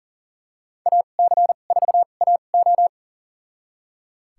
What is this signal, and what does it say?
Morse code, heard as a keyed tone, about 40 wpm, spaced as for 29 wpm, 700 Hz. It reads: AC4AO
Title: USA Call Signs - 5 Characters - 2x Character Spacing - 40wpm